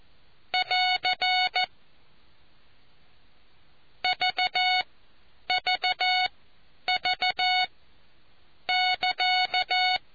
Morse code, heard as a keyed tone, about 14 wpm, spaced as for 8 wpm, 740 Hz. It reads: <AR> VVV<KA>